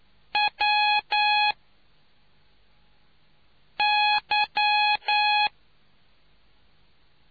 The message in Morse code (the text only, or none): W Y